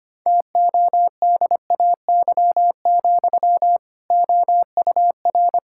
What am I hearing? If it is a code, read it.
Morse code, heard as a keyed tone, about 25 wpm, 700 Hz. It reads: TODAY, OUR
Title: Sentences from Top 1600 Words 25wpm